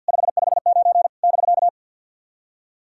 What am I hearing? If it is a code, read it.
Morse code, heard as a keyed tone, about 48 wpm, 700 Hz. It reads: 559 <BK>
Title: SOTA QSO Practice 50wpm